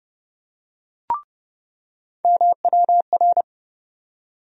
Morse code, heard as a keyed tone, about 30 wpm, 700 Hz. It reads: MWR
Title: USA Call Sign Postfixes - 3 Letters 30wpm